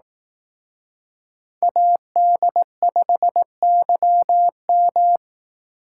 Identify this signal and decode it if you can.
Morse code, heard as a keyed tone, about 18 wpm, 700 Hz. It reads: AD5YM
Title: Speed Racing - USA Call Signs 15wpm